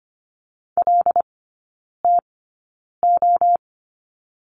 Morse code, heard as a keyed tone, about 25 wpm, spaced as for 9 wpm, 700 Hz. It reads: LTO
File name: course / 25:10wpm / 093 Lesson 16e - Letter c - Three Characters 25wpm